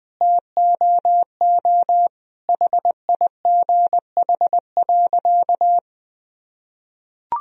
Morse code, heard as a keyed tone, about 20 wpm, 700 Hz. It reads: TOO HIGH.
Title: Sentences from Top 1000 Words - Review Part 1 - 20wpm